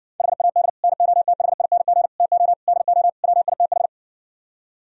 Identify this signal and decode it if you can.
Morse code, heard as a keyed tone, about 60 wpm, 700 Hz. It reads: HAD NOTHING TO DO WITH